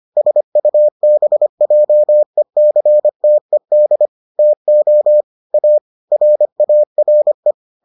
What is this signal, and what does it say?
Morse code, heard as a keyed tone, about 25 wpm, 600 Hz. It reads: SUBJECTED TO A RARE